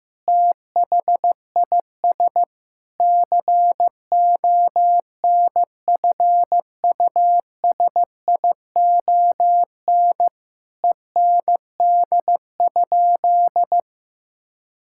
Morse code, heard as a keyed tone, about 15 wpm, 700 Hz. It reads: THIS CONFUSION END?
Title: Sentences from Top 1600 Words 15wpm